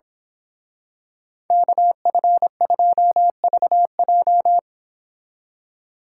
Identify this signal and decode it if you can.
Morse code, heard as a keyed tone, about 26 wpm, 700 Hz. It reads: KF2VJ
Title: Speed Racing - USA 2x2 Call Signs 22wpm